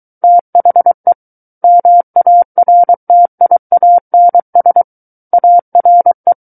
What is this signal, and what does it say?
Morse code, heard as a keyed tone, about 23 wpm, 700 Hz. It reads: THE MARTIANS ARE
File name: Book One / 14-Chapter 14 - 23 WPM